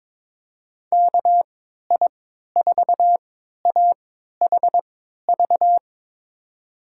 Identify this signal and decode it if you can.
Morse code, heard as a keyed tone, about 21 wpm, spaced as for 12 wpm, 700 Hz. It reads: KI4AHV
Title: USA Call Signs - 6 Characters - 3x Character Spacing - 22wpm